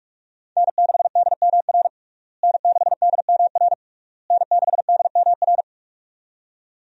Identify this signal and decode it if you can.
Morse code, heard as a keyed tone, about 46 wpm, 700 Hz. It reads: N6DMR N6DMR N6DMR
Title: USA Call Signs - 5 Characters - Repeated 3x - 45wpm